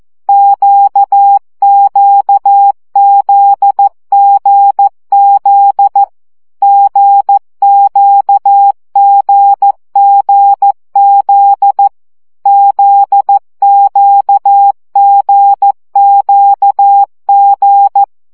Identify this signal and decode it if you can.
Morse code, heard as a keyed tone, about 14 wpm, 800 Hz. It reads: QQZGZ GQGGZ ZQGQG
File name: CW gyakorlas g z q 12 baud